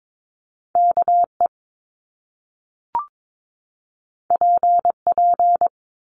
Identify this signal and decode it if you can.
Morse code, heard as a keyed tone, about 22 wpm, 700 Hz. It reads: KE PP
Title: Top Two Letter Combinations in Words 22wpm